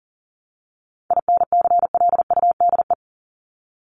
Morse code, heard as a keyed tone, about 39 wpm, 700 Hz. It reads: INCLUDE
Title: ED Suffix - Top 100 Words with Base Words 40wpm